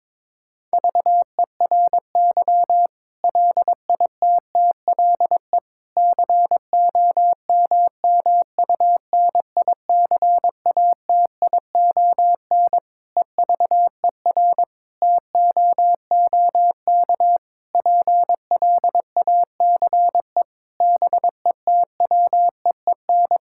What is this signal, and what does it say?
Morse code, heard as a keyed tone, about 22 wpm, 700 Hz. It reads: VERY LITTLE COMMUNICATION EVER TOOK PLACE BETWEEN
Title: Sentences from Top 1000 Words - Review Part 6 - 22wpm